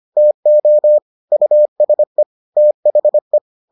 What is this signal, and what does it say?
Morse code, heard as a keyed tone, about 25 wpm, 600 Hz. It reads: TO USE THE